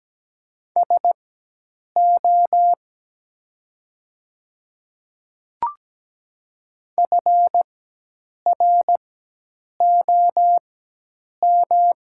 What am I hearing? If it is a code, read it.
Morse code, heard as a keyed tone, about 17 wpm, spaced as for 8 wpm, 700 Hz. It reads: SO FROM
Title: 4x Character Spacing - Top 100 Words - 17wpm